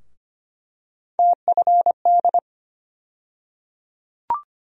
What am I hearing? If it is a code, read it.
Morse code, heard as a keyed tone, about 25 wpm, 700 Hz. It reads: TFD
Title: USA Call Sign Postfixes - 3 Letters 25wpm